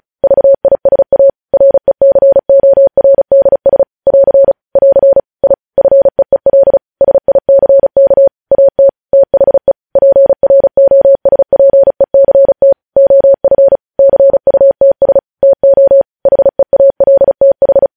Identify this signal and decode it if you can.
Morse code, heard as a keyed tone, about 36 wpm, 550 Hz. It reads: VISA RECORDS <AR> <AR> I FEEL SICK AT THE PROSPECT OF CUTS TO HEALTH
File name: podcast / News headlines-1x-35x35